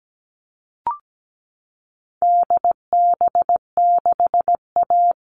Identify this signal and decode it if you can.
Morse code, heard as a keyed tone, about 17 wpm, 700 Hz. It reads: DB6A
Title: All Cut Numbers - 4-Digit 17wpm